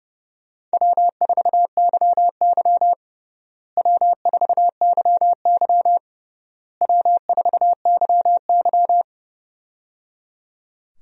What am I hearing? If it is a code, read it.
Morse code, heard as a keyed tone, about 30 wpm, 700 Hz. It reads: W4YY W4YY W4YY
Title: USA Call Signs - 4 Characters - Repeated 3x - 30wpm